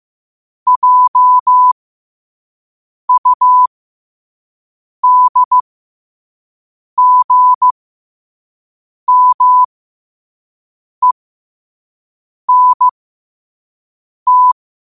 Morse code, heard as a keyed tone, about 15 wpm, spaced as for 5 wpm, 1000 Hz. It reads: JUDGMENT
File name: qotdfiles / 5wpm